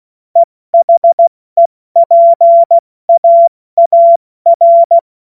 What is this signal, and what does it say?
Morse code, heard as a keyed tone, about 16 wpm, 680 Hz. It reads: EHEPAAR
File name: Morsekurs / Witze / Witz 15. August 2023 70 F80